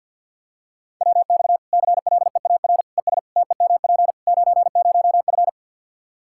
Morse code, heard as a keyed tone, about 53 wpm, 700 Hz. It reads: WX CLEAR ES TEMP 90F
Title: Practice QSO - Exchange 2 - Info - 50wpm